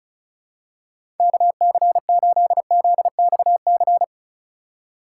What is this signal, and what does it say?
Morse code, heard as a keyed tone, about 34 wpm, 700 Hz. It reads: KC8ZXC
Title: USA Call Signs - 6 Characters - 3x Character Spacing - 35wpm